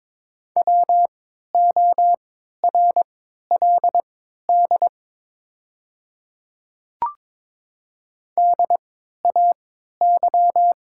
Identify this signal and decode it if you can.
Morse code, heard as a keyed tone, about 22 wpm, spaced as for 12 wpm, 700 Hz. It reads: WORLD DAY
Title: 3x Character Spacing - Top 200 Words - 22wpm